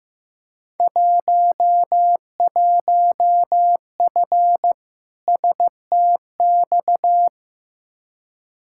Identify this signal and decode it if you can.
Morse code, heard as a keyed tone, about 15 wpm, 700 Hz. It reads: 11F STX
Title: Field Day Exchanges Encore - 15wpm